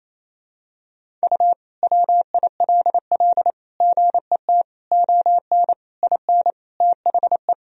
Morse code, heard as a keyed tone, about 28 wpm, 700 Hz. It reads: U WILL GET ON IN THE
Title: Sentences from Top 200 Words and Abbreviations 28wpm